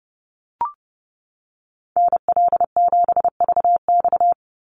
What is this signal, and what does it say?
Morse code, heard as a keyed tone, about 28 wpm, 700 Hz. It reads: NL7VX
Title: Call Signs - 30wpm